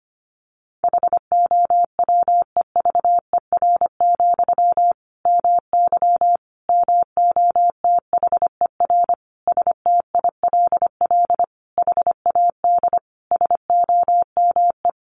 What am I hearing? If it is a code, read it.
Morse code, heard as a keyed tone, about 25 wpm, 700 Hz. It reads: HOWEVER, MY MOTHER STILL HAD SOME
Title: Sentences from Top 1000 Words 25wpm